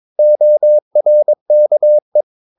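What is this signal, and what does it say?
Morse code, heard as a keyed tone, about 22 wpm, 600 Hz. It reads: ORKE
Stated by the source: Morse practice of short Danish words at 22 wpm, from CW med Gnister.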